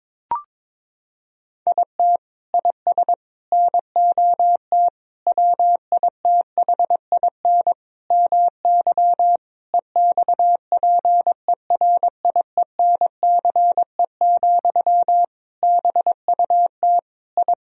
Morse code, heard as a keyed tone, about 22 wpm, 700 Hz. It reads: IT IS NOT WITHIN MY EXPERIENCE, BUT I
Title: Sentences from Top 1000 Words - Review Part 4 - 22wpm